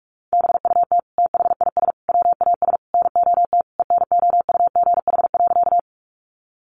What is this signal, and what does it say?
Morse code, heard as a keyed tone, about 44 wpm, 700 Hz. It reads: BUT THIS WAS NOT ENOUGH.
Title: Sentences from Top 600 Words 45wpm